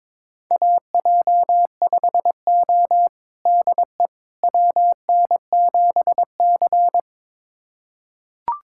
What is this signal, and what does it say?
Morse code, heard as a keyed tone, about 22 wpm, 700 Hz. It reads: AJ5O DE WN7C
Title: USA Callsign DE Callsign - 4 Characters - 22wpm